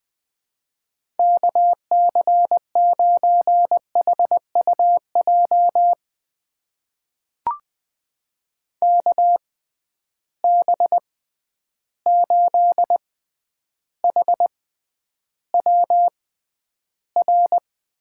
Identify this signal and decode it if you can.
Morse code, heard as a keyed tone, about 20 wpm, spaced as for 10 wpm, 700 Hz. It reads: KC9HUJ KB8HWR
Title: USA Call Signs - 6 Characters - 6x Character Spacing - 20wpm